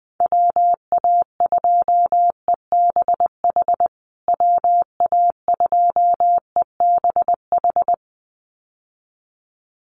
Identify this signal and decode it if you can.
Morse code, heard as a keyed tone, about 20 wpm, 700 Hz. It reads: WA2EBH WA2EBH
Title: USA Call Signs Repeated 2x 20wpm